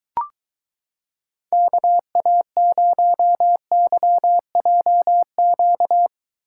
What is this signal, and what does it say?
Morse code, heard as a keyed tone, about 22 wpm, 700 Hz. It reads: KA0YJQ